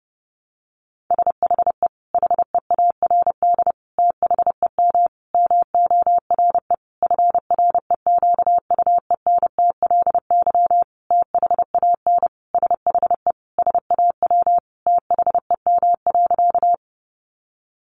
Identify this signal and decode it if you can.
Morse code, heard as a keyed tone, about 30 wpm, 700 Hz. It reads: SHE HEARD THEM MORE FREQUENTLY THAN SHE SAW THEM.
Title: Sentences from Top 1300 Words 30wpm